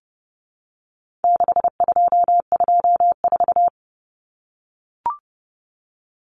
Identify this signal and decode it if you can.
Morse code, heard as a keyed tone, about 30 wpm, 700 Hz. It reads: 6224
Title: Common Cut Numbers - 4-Digit 30wpm